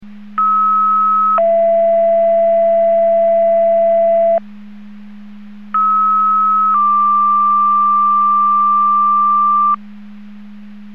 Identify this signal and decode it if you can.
Two-tone page, heard as a one-second tone, then a three-second tone, about 1290 then 670 Hz, about 1290 then 1180 Hz. It is heard from 0.4 to 9.8 s.
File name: Portage / 06163 07-04 Portage 154.385 Paging